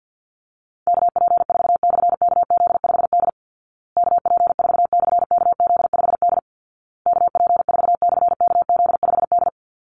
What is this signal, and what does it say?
Morse code, heard as a keyed tone, about 50 wpm, 700 Hz. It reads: KP4/KZ5D KP4/KZ5D KP4/KZ5D
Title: Call Signs Repeated 2x - 50wpm